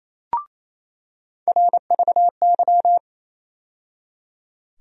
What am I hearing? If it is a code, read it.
Morse code, heard as a keyed tone, about 28 wpm, 700 Hz. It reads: RVY
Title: USA Call Sign Postfixes - 3 Letters 28wpm